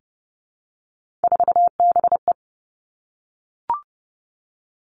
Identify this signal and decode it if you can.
Morse code, heard as a keyed tone, about 30 wpm, 700 Hz. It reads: 4BE
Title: Three Challenging Letter-Number Combinations 30wpm